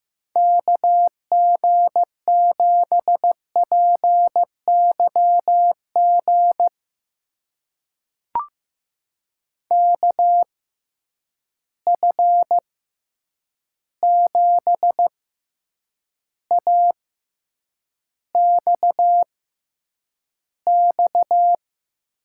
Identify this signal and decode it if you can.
Morse code, heard as a keyed tone, about 15 wpm, spaced as for 8 wpm, 700 Hz. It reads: KG7PYG KF7AXX
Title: USA Call Signs - 6 Characters - 6x Character Spacing - 15wpm